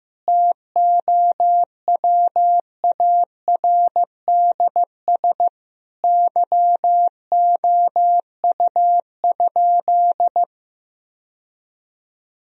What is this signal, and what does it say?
Morse code, heard as a keyed tone, about 16 wpm, 700 Hz. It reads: TOWARDS YOU?
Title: Sentences from Top 1000 Words - Review Part 2 - 15wpm